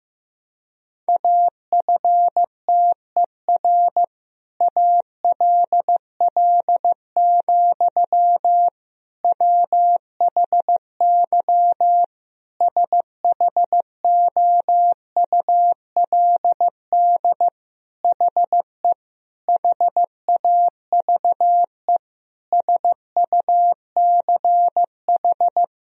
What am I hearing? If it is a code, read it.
Morse code, heard as a keyed tone, about 15 wpm, 700 Hz. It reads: AFTER ALL, WHY SHOULD HE HAVE SUCH
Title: Sentences from Top 1000 Words - Review Part 2 - 15wpm